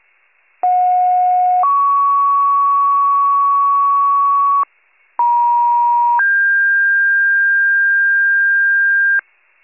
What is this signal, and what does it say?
Two-tone page, heard as a one-second tone, then a three-second tone, about 730 then 1120 Hz, about 950 then 1600 Hz, with noise. It is heard from 0.6 to 9.2 s.